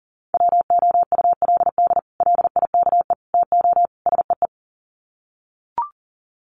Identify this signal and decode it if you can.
Morse code, heard as a keyed tone, about 37 wpm, 700 Hz. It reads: WOULD LIKE TO SEE
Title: Sets of 4 Words 40wpm